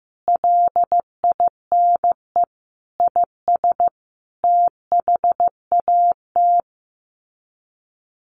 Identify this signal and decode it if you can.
Morse code, heard as a keyed tone, about 15 wpm, 700 Hz. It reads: LINE IS THAT
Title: Sets of 5 Words Encore 15wpm